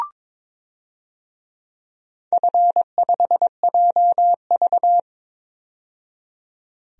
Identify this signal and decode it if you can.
Morse code, heard as a keyed tone, about 22 wpm, 700 Hz. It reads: F5JV
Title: DXCC-100 Call Signs - 4 Characters - 22wpm